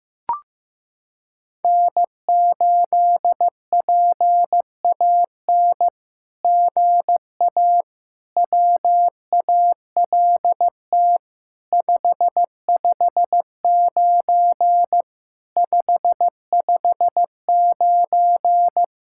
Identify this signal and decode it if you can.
Morse code, heard as a keyed tone, about 15 wpm, 700 Hz. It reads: N8PAN GA WALT 559 559
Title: SOTA QSO Practice 15wpm